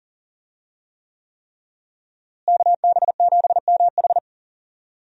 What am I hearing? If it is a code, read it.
Morse code, heard as a keyed tone, about 38 wpm, 700 Hz. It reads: KB7MH
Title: USA Call Signs - 5 Characters - Repeated 3x - 40wpm